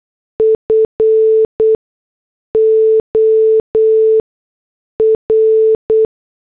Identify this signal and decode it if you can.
Morse code, heard as a keyed tone, about 8 wpm, 440 Hz. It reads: F O R
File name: morse / headlines